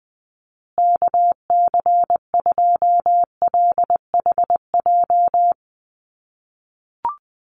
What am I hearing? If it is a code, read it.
Morse code, heard as a keyed tone, about 20 wpm, 700 Hz. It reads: KC2LHJ
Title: USA Call Signs Repeated 2x 20wpm